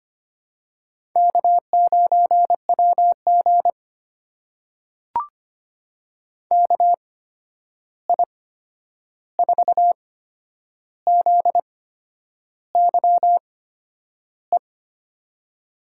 Morse code, heard as a keyed tone, about 25 wpm, spaced as for 9 wpm, 700 Hz. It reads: K9WG KI4ZYE
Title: USA Call Signs - 8x Character Spacing - 25wpm